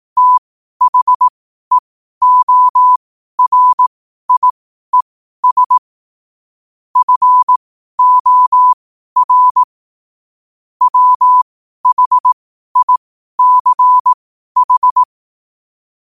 Morse code, heard as a keyed tone, about 18 wpm, spaced as for 12 wpm, 1000 Hz. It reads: THEORIES FOR WHICH
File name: qotdfiles / 12wpm